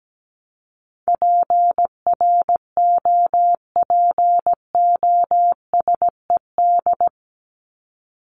Morse code, heard as a keyed tone, about 17 wpm, 700 Hz. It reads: PROPOSED
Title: ED Suffix - Top 100 Words with Base Words 17wpm